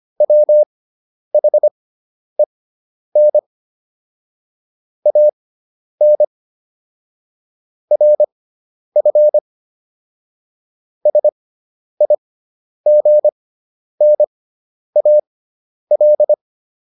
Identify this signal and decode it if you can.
Morse code, heard as a keyed tone, about 25 wpm, spaced as for 10 wpm, 600 Hz. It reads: WHEN AN RF SIGNAL